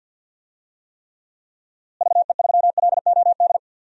Morse code, heard as a keyed tone, about 52 wpm, 700 Hz. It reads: VE3LOD
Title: DXCC-100 Call Signs - 6 Characters - Repeated 2x - 50wpm